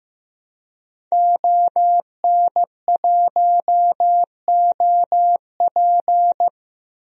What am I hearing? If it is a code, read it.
Morse code, heard as a keyed tone, about 16 wpm, 700 Hz. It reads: ON1OP
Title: DXCC-100 Call Signs - 5 Characters - 15wpm